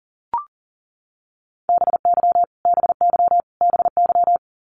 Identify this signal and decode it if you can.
Morse code, heard as a keyed tone, about 40 wpm, 700 Hz. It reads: BY BY BY
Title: Top 200 Words Repeated 40wpm